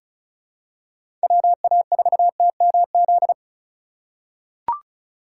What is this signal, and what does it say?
Morse code, heard as a keyed tone, about 34 wpm, 700 Hz. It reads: WA4TMZ
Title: USA Call Signs - 6 Characters - 3x Character Spacing - 35wpm